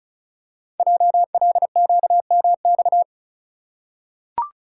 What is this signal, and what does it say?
Morse code, heard as a keyed tone, about 36 wpm, 690 Hz. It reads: JLQMX
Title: Code Groups - 5 Letters - 35wpm